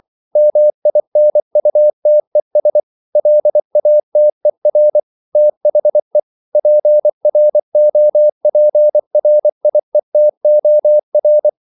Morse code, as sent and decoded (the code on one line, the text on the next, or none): -- .. -. ..- - . ... / .-.. .- - . .-. / - .... / .--. .-. --- .--. .-. .. - --- .-.
MINUTES LATER THE PROPRIETOR